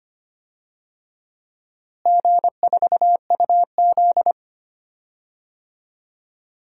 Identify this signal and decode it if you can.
Morse code, heard as a keyed tone, about 25 wpm, 700 Hz. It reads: G4UZ
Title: DXCC-100 Call Signs - 4 Characters - Repeated 2x - 25wpm